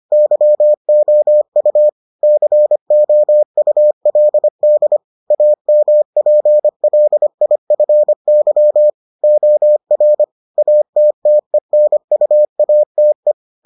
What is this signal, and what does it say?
Morse code, heard as a keyed tone, about 25 wpm, 600 Hz. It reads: YOU COULD AMPLIFY OR ATTENUATE